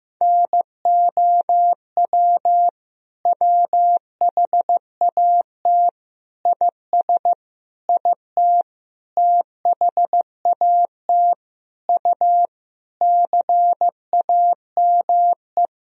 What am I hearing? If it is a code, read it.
Morse code, heard as a keyed tone, about 16 wpm, 700 Hz. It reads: NOW WHAT IS IT THAT U CAME
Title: Sentences from Top 200 Words and Abbreviations 15wpm